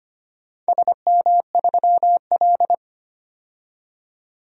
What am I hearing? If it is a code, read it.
Morse code, heard as a keyed tone, about 25 wpm, 700 Hz. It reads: SM3L
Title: DXCC-100 Call Signs - 4 Characters - 25wpm